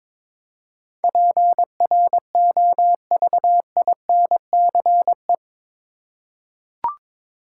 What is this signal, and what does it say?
Morse code, heard as a keyed tone, about 22 wpm, 700 Hz. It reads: PROVINCE
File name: practice / 22wpm / Eight Letter Words - 22wpm